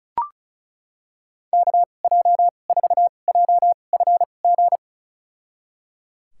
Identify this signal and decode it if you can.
Morse code, heard as a keyed tone, about 36 wpm, spaced as for 25 wpm, 700 Hz. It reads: KJ4JFG
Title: USA Call Signs - 6 Characters - 2x Character Spacing - 35wpm